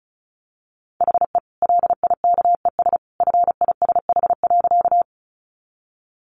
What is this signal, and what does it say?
Morse code, heard as a keyed tone, about 34 wpm, 700 Hz. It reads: HE LIKES FISH.
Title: Easy Phrases 35wpm